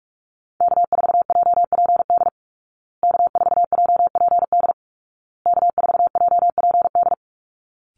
Morse code, heard as a keyed tone, about 44 wpm, 700 Hz. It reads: K4JPD K4JPD K4JPD
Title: Call Signs Repeated 2x - 45wpm